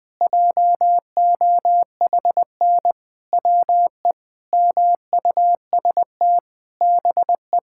Morse code, heard as a keyed tone, about 20 wpm, 700 Hz. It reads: JOHN WE MUST BE
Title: Sentences with 5-Word Phrases - 20wpm